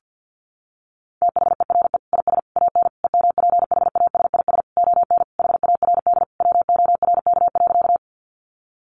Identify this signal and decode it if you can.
Morse code, heard as a keyed tone, about 50 wpm, 700 Hz. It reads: THERE IS AN EMPHASIS ON HARD WORK.
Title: Sentences from Top 1400 Words 50wpm